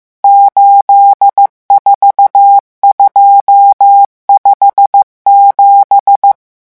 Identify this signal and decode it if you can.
Morse code, heard as a keyed tone, about 15 wpm, 800 Hz. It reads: 84257
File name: morse mp3 / revision11 chiffres vitesse 14